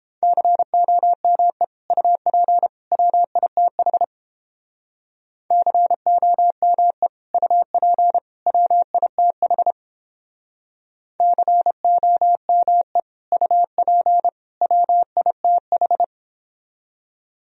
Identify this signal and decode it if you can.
Morse code, heard as a keyed tone, about 29 wpm, 700 Hz. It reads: COME UP WITH COME UP WITH COME UP WITH